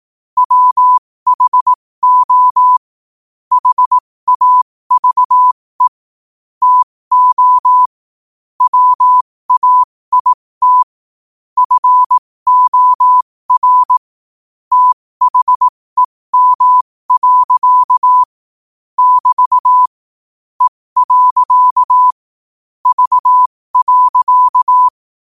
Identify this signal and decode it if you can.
Morse code, heard as a keyed tone, about 18 wpm, spaced as for 15 wpm, 1000 Hz. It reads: WHO HAVE TO WAIT FOR THEM. <BT> E. V.
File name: qotdfiles / 15wpm